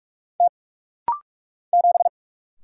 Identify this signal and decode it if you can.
Morse code, heard as a keyed tone, about 44 wpm, 700 Hz. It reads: T 7
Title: Single Letter-Number Alt - Rapid-Fire 45wpm